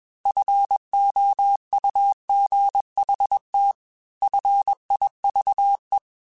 Morse code, heard as a keyed tone, about 21 wpm, 800 Hz. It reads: FOUGHT FIVE
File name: F Runs / F Runs mp3s / F Runs @20